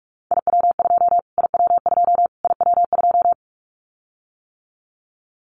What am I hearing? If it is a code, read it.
Morse code, heard as a keyed tone, about 44 wpm, 700 Hz. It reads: IW2 IW2 IW2
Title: Callsign Prefixes for Top 25 DXCC - Repeated 3x - 45wpm